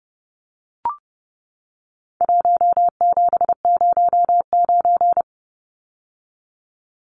Morse code, heard as a keyed tone, about 30 wpm, 700 Hz. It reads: 1709
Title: Numbers - Contest Style - 4-Digit 30wpm